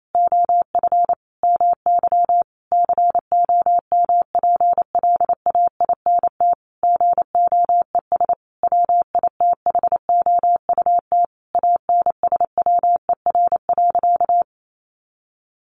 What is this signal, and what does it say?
Morse code, heard as a keyed tone, about 28 wpm, 700 Hz. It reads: OF MY COMPLAINT GOES WITHOUT ANSWER.
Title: Sentences from Top 1500 Words 28wpm